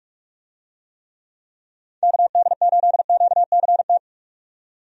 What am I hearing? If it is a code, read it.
Morse code, heard as a keyed tone, about 46 wpm, 700 Hz. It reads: KD8QCT
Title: USA Call Signs - 6 Characters - 45wpm